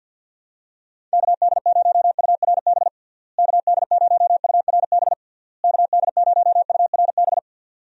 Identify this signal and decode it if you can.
Morse code, heard as a keyed tone, about 45 wpm, 700 Hz. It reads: KD0URB KD0URB KD0URB
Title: USA Call Signs - 6 Characters - Repeated 3x - 50wpm